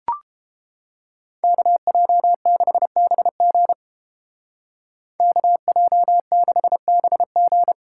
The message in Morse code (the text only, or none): KJ6BG KJ6BG